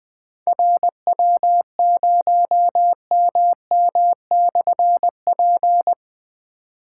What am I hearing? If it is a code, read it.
Morse code, heard as a keyed tone, about 20 wpm, 700 Hz. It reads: RW0MM/P
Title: Difficult Call Signs 20wpm